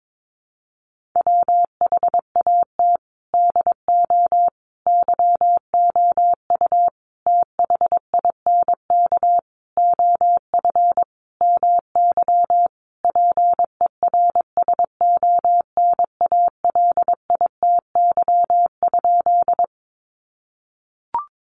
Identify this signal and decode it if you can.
Morse code, heard as a keyed tone, about 22 wpm, 700 Hz. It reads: WHAT DO YOU THINK OF MY PERSONALITY?
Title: Sentences from Top 1400 Words 22wpm